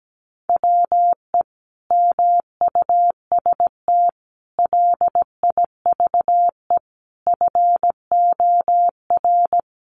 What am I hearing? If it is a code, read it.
Morse code, heard as a keyed tone, about 17 wpm, 700 Hz. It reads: WE MUST LIVE FOR
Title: Sentences from Top 700 Words 17wpm